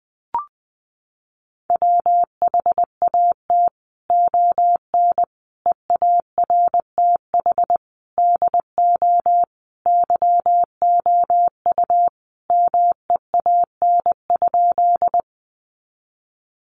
Morse code, heard as a keyed tone, about 20 wpm, 700 Hz. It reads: WHAT ON EARTH DO YOU MEAN?
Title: Sentences from Top 200 Words 20wpm